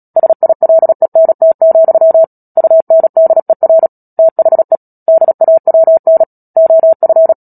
Morse code, heard as a keyed tone, about 37 wpm, 650 Hz. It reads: SILENT, UNDER THE DAWN OF
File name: Book Two / 10-Chapter 10 - Epilogue - 36 WPM